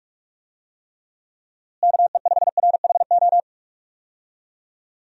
Morse code, heard as a keyed tone, about 46 wpm, 700 Hz. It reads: KE5RHO
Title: USA Call Signs - 6 Characters - Repeated 3x - 45wpm